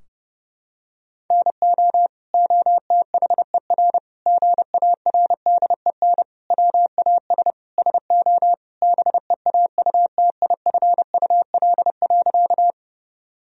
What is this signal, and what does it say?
Morse code, heard as a keyed tone, about 32 wpm, 700 Hz. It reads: NO OTHER GARDEN WAS SO BEAUTIFUL.
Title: Sentences from Top 1000 Words - Review Part 4 - 30wpm